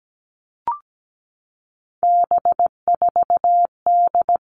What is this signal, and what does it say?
Morse code, heard as a keyed tone, about 17 wpm, 700 Hz. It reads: B4D
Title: All Cut Numbers - 3-Digit 17wpm